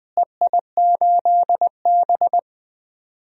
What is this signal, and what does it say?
Morse code, heard as a keyed tone, about 20 wpm, 700 Hz. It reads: EI8B
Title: DXCC-100 Call Signs - 4 Characters - Repeated 3x - 20wpm